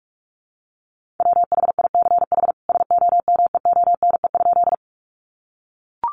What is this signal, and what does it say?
Morse code, heard as a keyed tone, about 44 wpm, 700 Hz. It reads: WHICH SOMEONE?
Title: Sentences from Top 600 Words 45wpm